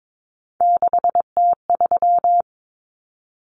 Morse code, heard as a keyed tone, about 22 wpm, 700 Hz. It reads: 6T3
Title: Common Cut Numbers - 3-Digit 22wpm